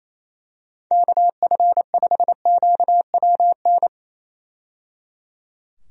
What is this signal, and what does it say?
Morse code, heard as a keyed tone, about 27 wpm, 700 Hz. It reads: KF5QWN
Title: USA Call Signs - 6 Characters - 28wpm